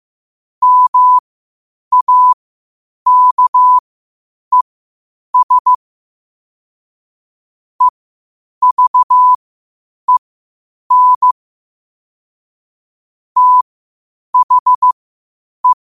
Morse code, heard as a keyed tone, about 15 wpm, spaced as for 8 wpm, 1000 Hz. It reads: MAKES EVEN THE
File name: qotdfiles / 8wpm